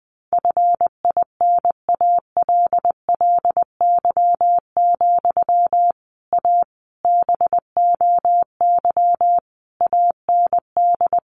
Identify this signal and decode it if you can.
Morse code, heard as a keyed tone, about 20 wpm, 700 Hz. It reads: FINALLY, A BOY AND
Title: Sentences from Top 1200 Words 20wpm